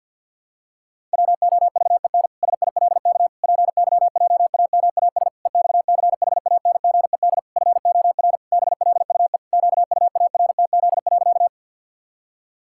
Morse code, heard as a keyed tone, about 48 wpm, 700 Hz. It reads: WOVEN SILK PYJAMAS EXCHANGED FOR BLUE QUARTZ.
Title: Pangram Challenge v2 - 50wpm